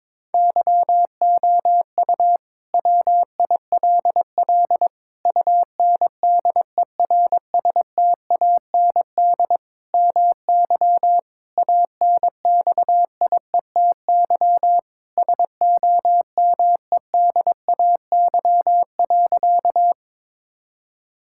Morse code, heard as a keyed tone, about 22 wpm, 700 Hz. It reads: YOU WILL UNDERSTAND MY ANXIETY SOMEDAY.
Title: Sentences from Top 1600 Words 22wpm